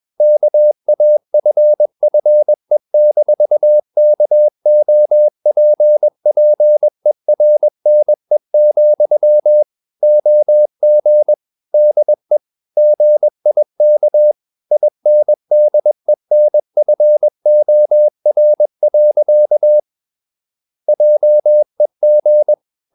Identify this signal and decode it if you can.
Morse code, heard as a keyed tone, about 21 wpm, 600 Hz. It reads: KAFFE-KOPPERNE, OG DE GIK INDENFOR. JEG
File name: Ferie QRS 21wpm